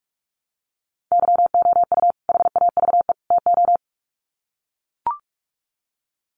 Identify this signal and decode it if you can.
Morse code, heard as a keyed tone, about 44 wpm, 700 Hz. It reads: YOU HAVE TO
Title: Sets of 3 Words 45wpm